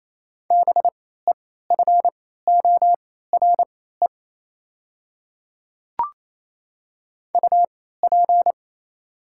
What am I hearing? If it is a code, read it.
Morse code, heard as a keyed tone, about 28 wpm, spaced as for 16 wpm, 700 Hz. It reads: BEFORE UP
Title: 3x Character Spacing - Top 500 Words - 28wpm